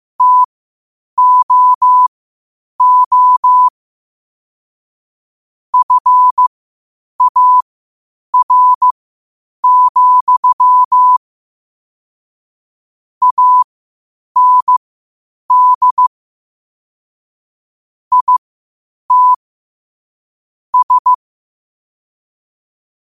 Morse code, heard as a keyed tone, about 15 wpm, spaced as for 8 wpm, 1000 Hz. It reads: TOO FAR, AND IT S
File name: qotdfiles / 8wpm